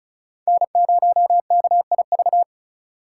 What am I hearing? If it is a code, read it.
Morse code, heard as a keyed tone, about 36 wpm, 700 Hz. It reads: N0KIV
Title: USA Call Signs - 5 Characters - 2x Character Spacing - 35wpm